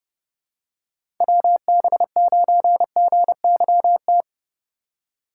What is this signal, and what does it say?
Morse code, heard as a keyed tone, about 31 wpm, 700 Hz. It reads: WB9GYT
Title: USA Call Signs - 6 Characters - 2x Character Spacing - 30wpm